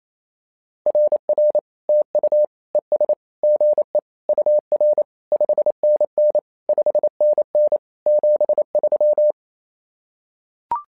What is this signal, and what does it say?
Morse code, heard as a keyed tone, about 27 wpm, 600 Hz. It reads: RR TU ES GE UR 5NN 5NN 73
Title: SOTA QSO Practice 28wpm